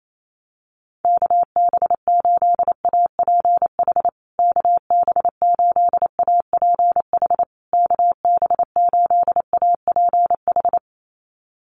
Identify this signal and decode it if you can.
Morse code, heard as a keyed tone, about 28 wpm, 690 Hz. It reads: KB8APH KB8APH KB8APH
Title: USA Call Signs Repeated 3x 28wpm